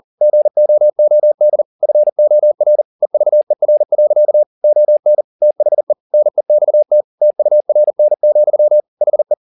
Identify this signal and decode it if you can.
Morse code, heard as a keyed tone, about 40 wpm, 600 Hz. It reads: GOOD FOR EVER. ON THE NEXT TURN, HE
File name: the old man and the sea 2 40wpm